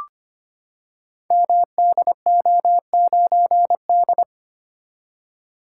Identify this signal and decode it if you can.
Morse code, heard as a keyed tone, about 25 wpm, 700 Hz. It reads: MDO9D
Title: Code Groups - 5 Letter:Numbers - 25wpm